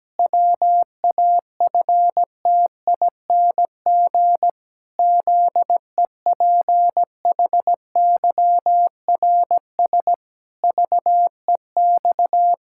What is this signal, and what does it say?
Morse code, heard as a keyed tone, about 17 wpm, 700 Hz. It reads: WAFTING ZEPHYRS VEX
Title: Pangram Challenge v2 - 17wpm